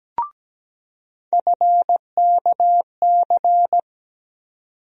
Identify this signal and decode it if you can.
Morse code, heard as a keyed tone, about 17 wpm, 700 Hz. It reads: FKC
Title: USA Call Sign Postfixes - 3 Letters 17wpm